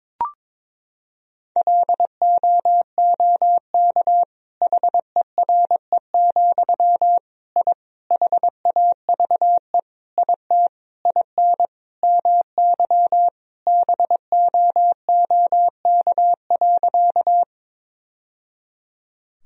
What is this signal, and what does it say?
Morse code, heard as a keyed tone, about 21 wpm, 700 Hz. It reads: LOOK HERE, I HAVE IT IN MY BOOK.
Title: Sentences from Top 1000 Words - Review Part 1 - 22wpm